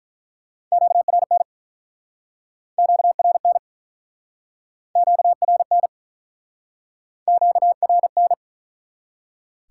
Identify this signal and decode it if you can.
Morse code, heard as a keyed tone, about 43 wpm, 700 Hz. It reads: QRN QRN QRN QRN
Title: Speed-Racing Top 180 QSO Elements 35wpm